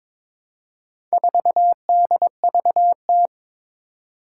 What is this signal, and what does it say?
Morse code, heard as a keyed tone, about 23 wpm, 700 Hz. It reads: 4DVT
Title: Four Challenging Letter-Number Combinations 22wpm